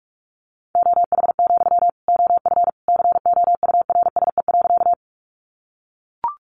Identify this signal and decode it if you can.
Morse code, heard as a keyed tone, about 44 wpm, 700 Hz. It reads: OH, OF COURSE.
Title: Sentences from Top 500 Words 45wpm